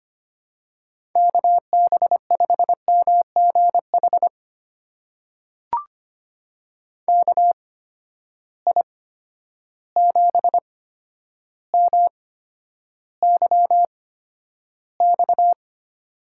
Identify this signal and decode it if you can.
Morse code, heard as a keyed tone, about 25 wpm, spaced as for 11 wpm, 700 Hz. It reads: KB5MGH KI7MYX
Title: USA Call Signs - 6 Characters - 8x Character Spacing - 25wpm